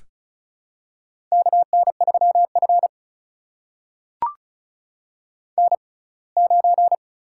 Morse code, heard as a keyed tone, about 36 wpm, spaced as for 24 wpm, 700 Hz. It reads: KN3F N9
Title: USA Call Signs - 4 Characters - 6x Character Spacing - 35wpm